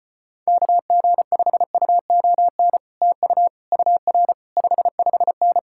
Morse code, heard as a keyed tone, about 33 wpm, 700 Hz. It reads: KG5UON TU UR 55N